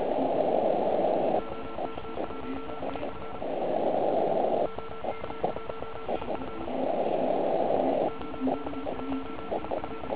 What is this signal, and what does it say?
Morse code, heard as a keyed tone, about 35 wpm, 1340 Hz, with noise. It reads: OK1IF OK1IF OK1IF
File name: Archiv / CUC129 / CUC129 pileup